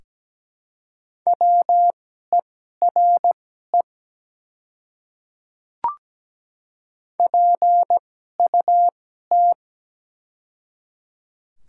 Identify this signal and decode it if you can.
Morse code, heard as a keyed tone, about 17 wpm, spaced as for 12 wpm, 700 Hz. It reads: WERE PUT
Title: 2x Character Spacing - Top 200 Words - 17wpm